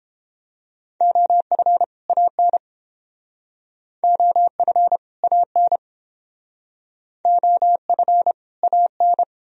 Morse code, heard as a keyed tone, about 31 wpm, 700 Hz. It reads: OF AN OF AN OF AN